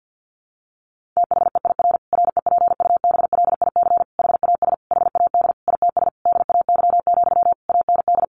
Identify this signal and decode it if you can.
Morse code, heard as a keyed tone, about 48 wpm, 700 Hz. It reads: THEIR REPUBLIC HAS HAD ITS DAY, AND
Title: Sentences from Top 1500 Words 50wpm